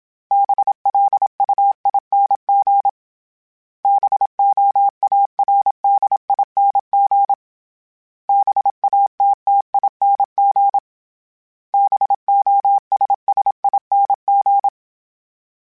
Morse code, heard as a keyed tone, about 27 wpm, 800 Hz. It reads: BLUING BOARDING BATTING BOSSING
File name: Special INGs-mp3s / Special INGs-mp3s / Sngl Ing @25